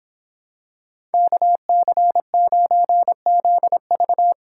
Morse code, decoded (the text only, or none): KC9ZV